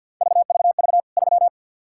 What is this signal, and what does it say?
Morse code, heard as a keyed tone, about 52 wpm, 700 Hz. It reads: VVV 3
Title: K1USN SST QSO Practice 50wpm